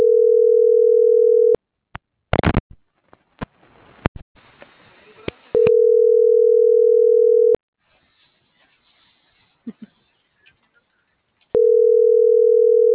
Ringback tone (North America), 440 + 480 Hz.